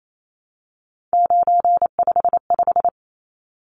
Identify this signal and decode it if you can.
Morse code, heard as a keyed tone, about 28 wpm, 700 Hz. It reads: 955
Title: Numbers - 3-Digit 28wpm